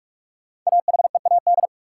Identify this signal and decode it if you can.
Morse code, heard as a keyed tone, about 44 wpm, 700 Hz. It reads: AHEAD
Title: Sentences from Top 1000 Words - Review Part 6 - New Word Drill - 45wpm